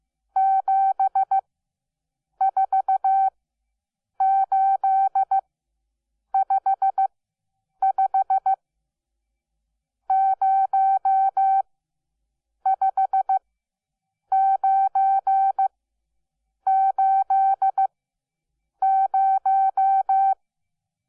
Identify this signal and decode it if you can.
Morse code, heard as a keyed tone, about 15 wpm, spaced as for 7 wpm, 770 Hz. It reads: 74855 05980
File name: Onlinelehrgang / morsen / DJ4UF Morsen12